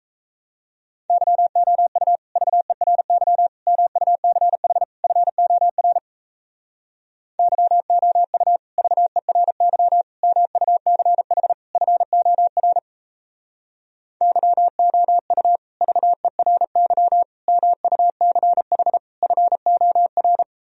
Morse code, heard as a keyed tone, about 37 wpm, 700 Hz. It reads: YOU VERY MUCH FOR YOU VERY MUCH FOR YOU VERY MUCH FOR